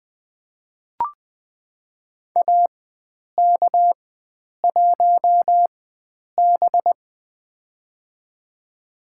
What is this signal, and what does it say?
Morse code, heard as a keyed tone, about 20 wpm, spaced as for 9 wpm, 700 Hz. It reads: AK1B
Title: USA Call Signs - 4 Characters - 4x Character Spacing - 20wpm